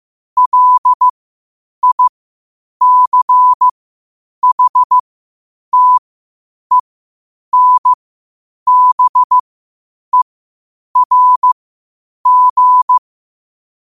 Morse code, heard as a keyed tone, about 15 wpm, spaced as for 8 wpm, 1000 Hz. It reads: LICHTENBERG
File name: qotdfiles / 8wpm